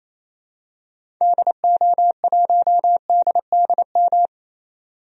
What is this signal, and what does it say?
Morse code, heard as a keyed tone, about 28 wpm, 700 Hz. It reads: DO1DDM
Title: DXCC-100 Call Signs - 6 Characters - 28wpm